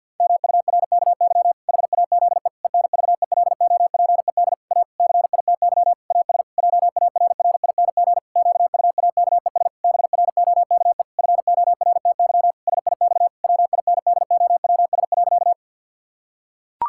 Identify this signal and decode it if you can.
Morse code, heard as a keyed tone, about 52 wpm, 700 Hz. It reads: MURKY HAZE ENVELOPED A CITY AS JARRING QUAKES BROKE FORTY SIX WINDOWS.
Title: Pangram Challenge v2 - 50wpm